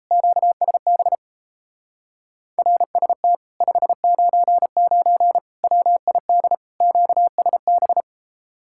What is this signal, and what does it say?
Morse code, heard as a keyed tone, about 38 wpm, 700 Hz. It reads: QSB RST 599 WID QSB